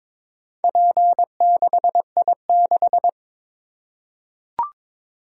Morse code, heard as a keyed tone, about 22 wpm, 700 Hz. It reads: P6I6
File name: practice / 22wpm / Four Letter-Number Combinations - 22wpm